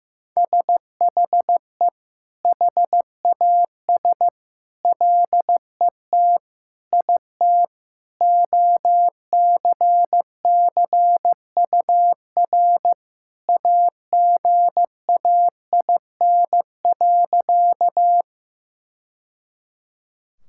Morse code, heard as a keyed tone, about 16 wpm, 700 Hz. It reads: SHE HAS LET IT OCCUR AGAIN.
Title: Sentences from Top 1000 Words - Review Part 6 - 15wpm